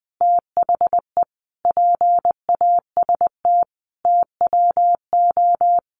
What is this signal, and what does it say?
Morse code, heard as a keyed tone, about 20 wpm, 700 Hz. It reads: THE PAST TWO
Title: Sets of 5 Words 20wpm